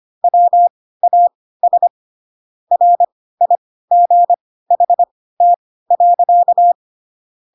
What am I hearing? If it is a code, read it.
Morse code, heard as a keyed tone, about 25 wpm, spaced as for 16 wpm, 700 Hz. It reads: WAS RIGHT.
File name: Pinocchio / Pinocchio ch5 10 25 ik1hge